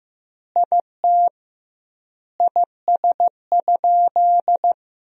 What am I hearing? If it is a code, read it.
Morse code, heard as a keyed tone, about 15 wpm, 700 Hz. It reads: IT IS?
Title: 2x Word-Spacing - Sentences from Top 100 Words - 15wpm